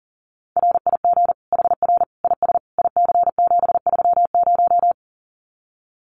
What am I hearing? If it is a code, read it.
Morse code, heard as a keyed tone, about 40 wpm, 700 Hz. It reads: RIG HR IS IC730
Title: Common QSO Phrases 40wpm